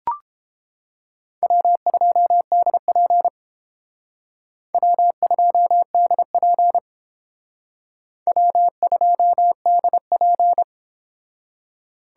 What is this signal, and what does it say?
Morse code, heard as a keyed tone, about 30 wpm, 700 Hz. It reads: W2DP W2DP W2DP